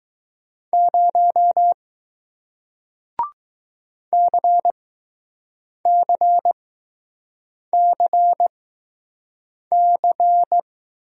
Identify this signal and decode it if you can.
Morse code, heard as a keyed tone, about 20 wpm, spaced as for 6 wpm, 700 Hz. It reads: 0 CCCC